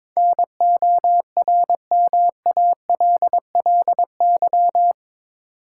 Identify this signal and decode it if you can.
Morse code, heard as a keyed tone, about 22 wpm, 700 Hz. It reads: NORMALLY
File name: practice / 22wpm / Eight Letter Words - 22wpm